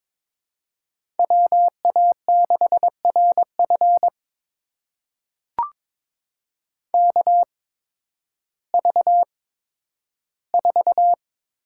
Morse code, heard as a keyed tone, about 22 wpm, spaced as for 12 wpm, 700 Hz. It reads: WA6RF KV4
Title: USA Call Signs - 5 Characters - 8x Character Spacing - 22wpm